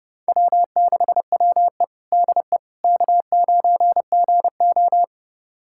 Morse code, heard as a keyed tone, about 32 wpm, 700 Hz. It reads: W6WE DE K9GO
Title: USA Callsign DE Callsign - 4 Characters - 30wpm